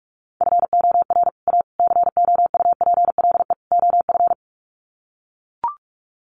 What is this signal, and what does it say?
Morse code, heard as a keyed tone, about 44 wpm, 700 Hz. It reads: FOR A COUPLE OF
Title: Sets of 4 Words 45wpm